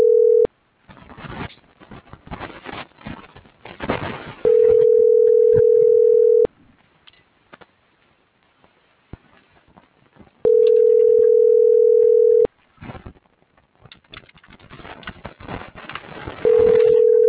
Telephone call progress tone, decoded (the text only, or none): ringback tone